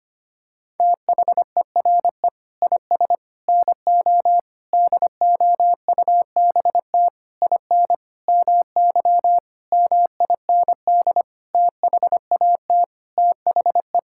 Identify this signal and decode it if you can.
Morse code, heard as a keyed tone, about 25 wpm, 700 Hz. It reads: THERE IS NO DOUBT IN MY MIND THAT THE
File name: practice / 25wpm / Sentences with 9-Word Phrases - 25wpm